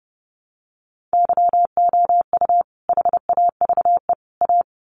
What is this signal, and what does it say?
Morse code, heard as a keyed tone, about 29 wpm, 700 Hz. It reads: YOU HAVE A
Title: Sets of 3 Words 30wpm